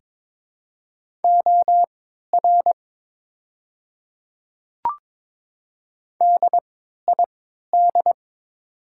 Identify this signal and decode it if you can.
Morse code, heard as a keyed tone, about 22 wpm, spaced as for 12 wpm, 700 Hz. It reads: OR DID
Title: 3x Character Spacing - Top 100 Words - 22wpm